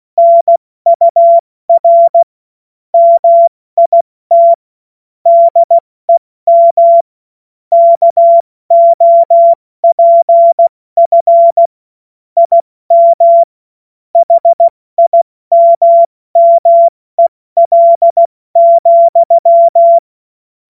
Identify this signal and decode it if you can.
Morse code, heard as a keyed tone, about 16 wpm, 680 Hz. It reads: NUR MIT DEM KOPF IM HIMMEL,
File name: Morsekurs / Witze / Witz 09. Dezember 2025 70 F80